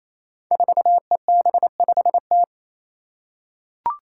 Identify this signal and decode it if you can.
Morse code, heard as a keyed tone, about 28 wpm, 700 Hz. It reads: 4EB5T
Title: Five Challenging Letter-Number Combinations 28wpm